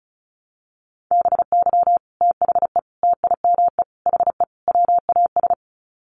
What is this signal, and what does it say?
Morse code, heard as a keyed tone, about 34 wpm, 700 Hz. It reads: BY THE TIME HE WAS
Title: Sets of 5 Words Encore 35wpm